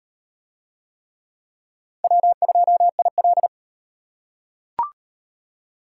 Morse code, heard as a keyed tone, about 35 wpm, 700 Hz. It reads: W2IL